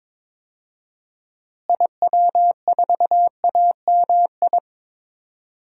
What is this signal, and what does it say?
Morse code, heard as a keyed tone, about 22 wpm, 700 Hz. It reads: IW4AMI
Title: DXCC-100 Call Signs - 6 Characters - Repeated 3x - 22wpm